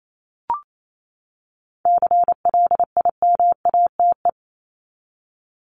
Morse code, heard as a keyed tone, about 28 wpm, 700 Hz. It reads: CLIMATE
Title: Top 1501-1750 Words 28wpm